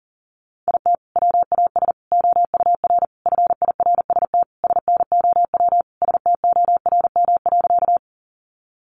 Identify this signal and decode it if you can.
Morse code, heard as a keyed tone, about 37 wpm, 700 Hz. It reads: IT WAS OUR FIRST SNOW STORM.
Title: Sentences from Top 1500 Words 40wpm